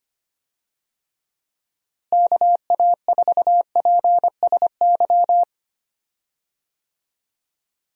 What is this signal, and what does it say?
Morse code, heard as a keyed tone, about 25 wpm, 700 Hz. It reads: KA4PSY
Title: USA Call Signs - 6 Characters - Repeated 2x - 25wpm